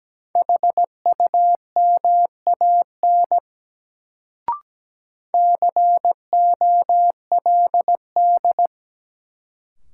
Morse code, heard as a keyed tone, about 17 wpm, 700 Hz. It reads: HUMAN COLD
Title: Sets of 5 Words - New Word Drill - 17wpm